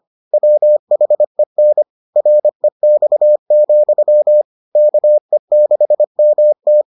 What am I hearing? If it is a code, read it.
Morse code, heard as a keyed tone, about 25 wpm, 600 Hz. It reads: WHEN REX, KE6MT